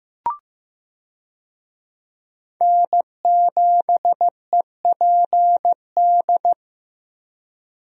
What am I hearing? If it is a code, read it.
Morse code, heard as a keyed tone, about 15 wpm, 700 Hz. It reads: N7EPD
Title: USA Call Signs - 5 Characters - Repeated 2x - 15wpm